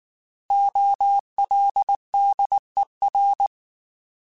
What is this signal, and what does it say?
Morse code, heard as a keyed tone, about 19 wpm, 800 Hz. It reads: OLDER
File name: Alphabetical / Alphabetical mp3s / Os @18